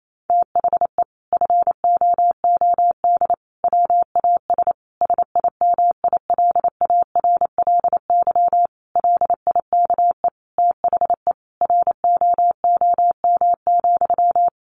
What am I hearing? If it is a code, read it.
Morse code, heard as a keyed tone, about 28 wpm, 700 Hz. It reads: THE FOOD WAS SIMILARLY LIKE THE ROOM,
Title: Sentences from Top 1400 Words 28wpm